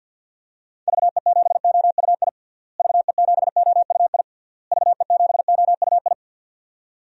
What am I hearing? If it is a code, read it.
Morse code, heard as a keyed tone, about 52 wpm, 700 Hz. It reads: VE7OUI VE7OUI VE7OUI
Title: DXCC-100 Call Signs - 6 Characters - Repeated 3x - 50wpm